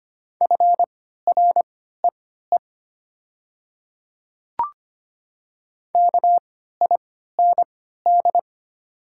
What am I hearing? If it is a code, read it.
Morse code, heard as a keyed tone, about 25 wpm, spaced as for 14 wpm, 700 Hz. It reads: FREE KIND
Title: 3x Character Spacing - Top 500 Words - 25wpm